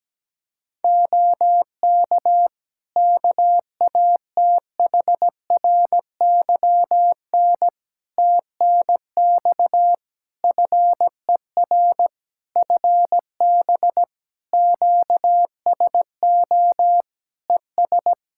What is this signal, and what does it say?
Morse code, heard as a keyed tone, about 17 wpm, 700 Hz. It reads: OK KATHRYN TNX FER FB QSO ES
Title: K1USN SST QSO Practice 17wpm